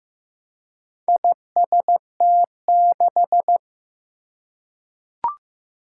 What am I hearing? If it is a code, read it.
Morse code, heard as a keyed tone, about 15 wpm, 700 Hz. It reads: IST6
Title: Four Challenging Letter-Number Combinations 15wpm